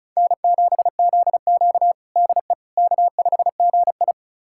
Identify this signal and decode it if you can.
Morse code, heard as a keyed tone, about 36 wpm, 700 Hz. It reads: N7ZQ DE K5GI
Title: USA Callsign DE Callsign - 4 Characters - 35wpm